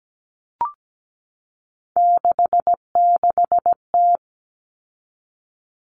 Morse code, heard as a keyed tone, about 17 wpm, 700 Hz. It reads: 66T
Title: Common Cut Numbers - 3-Digit 17wpm